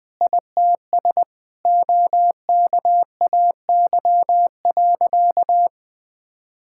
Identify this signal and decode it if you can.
Morse code, heard as a keyed tone, about 21 wpm, 700 Hz. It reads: ITS OKAY.
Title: Sentences with 5-Word Phrases - 20wpm